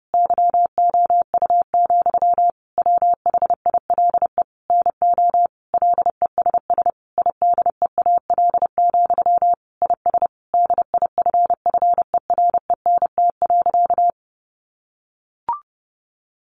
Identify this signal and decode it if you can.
Morse code, heard as a keyed tone, about 30 wpm, 700 Hz. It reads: YOU, WHILE NO LESS IDEAL, IS DIFFERENT.
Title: Sentences from Top 1400 Words 30wpm